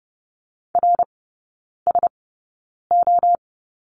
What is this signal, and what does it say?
Morse code, heard as a keyed tone, about 30 wpm, spaced as for 9 wpm, 700 Hz. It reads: RSO
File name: course / 30:10wpm / 087 Lesson 15e - Number 5 - Three Characters 30wpm